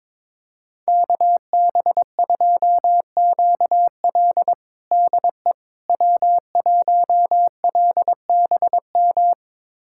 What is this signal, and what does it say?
Morse code, heard as a keyed tone, about 22 wpm, 700 Hz. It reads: KB2QL DE W1LBM
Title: USA Callsign DE Callsign - 5 Characters - 22wpm